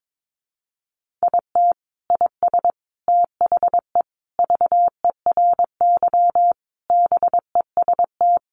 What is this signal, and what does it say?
Morse code, heard as a keyed tone, about 22 wpm, 700 Hz. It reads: IT IS THE VERY BEST
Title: Sentences from Top 1300 Words 22wpm